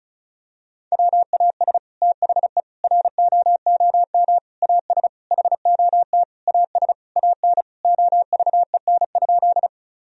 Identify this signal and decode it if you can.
Morse code, heard as a keyed tone, about 34 wpm, 700 Hz. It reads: WAS THE ROOM AS HOT AS AN OVEN?
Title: Sentences from Top 1000 Words - Review Part 4 - 35wpm